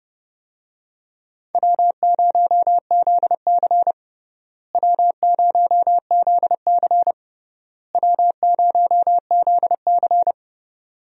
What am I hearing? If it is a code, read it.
Morse code, heard as a keyed tone, about 30 wpm, 700 Hz. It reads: W0ZC W0ZC W0ZC
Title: USA Call Signs - 4 Characters - Repeated 3x - 30wpm